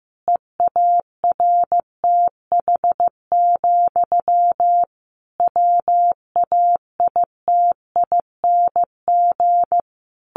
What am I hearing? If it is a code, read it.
Morse code, heard as a keyed tone, about 15 wpm, 700 Hz. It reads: EARTH, WAITING
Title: A Sound Of Thunder 15wpm